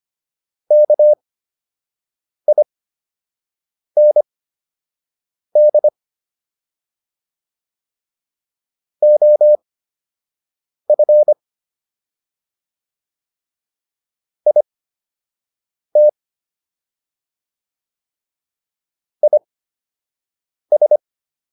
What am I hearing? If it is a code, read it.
Morse code, heard as a keyed tone, about 25 wpm, spaced as for 6 wpm, 600 Hz. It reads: KIND OF IT IS